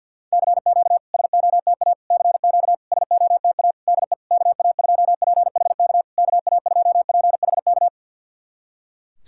Morse code, heard as a keyed tone, about 53 wpm, 690 Hz. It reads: CQ SOTA CQ SOTA DE KA2PHK KA2PHK
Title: SOTA QSO Practice 50wpm